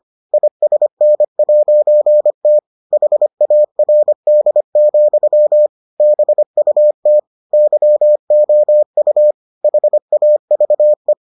Morse code, heard as a keyed tone, about 25 wpm, 600 Hz. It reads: ISN'T HARD, BUT YOU HAVE